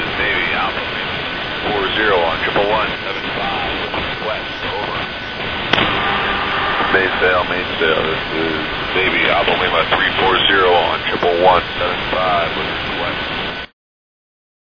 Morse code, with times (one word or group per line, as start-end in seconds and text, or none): none